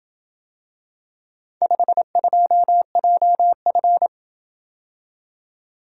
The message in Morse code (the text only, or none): 52JF